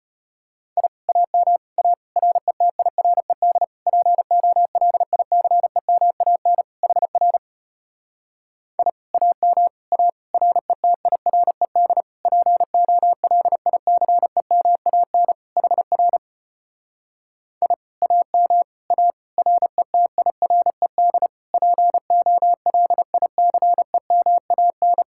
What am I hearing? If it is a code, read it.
Morse code, heard as a keyed tone, about 33 wpm, 700 Hz. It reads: I AM A RETIRED POLICEMAN HR I AM A RETIRED POLICEMAN HR I AM A RETIRED POLICEMAN